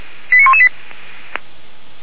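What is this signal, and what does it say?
Five-tone selective call (ZVEI), heard as quick runs of five tones, tones of about 70 ms. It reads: 89118
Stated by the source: ZVEI five-tone selective call received on SDR, feuerwehr-orlamuende fire brigade.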